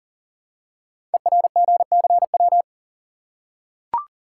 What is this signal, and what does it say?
Morse code, heard as a keyed tone, about 38 wpm, 700 Hz. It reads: ERGCW
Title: Code Groups - 5 Letter:Numbers - 40wpm